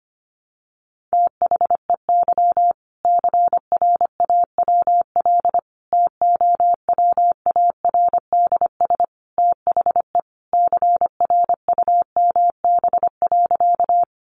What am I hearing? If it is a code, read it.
Morse code, heard as a keyed tone, about 25 wpm, 700 Hz. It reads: THEY CRAWL TOWARDS THE CRUMB.
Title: Additional Easy Phrases 25wpm